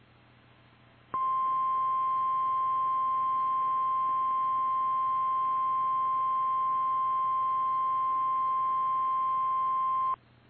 Emergency Alert System (SAME) tones (the 1050 Hz attention tone). It is heard from 1.1 to 10.2 s.
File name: eas / KKPRKXPN / KKPRKXPN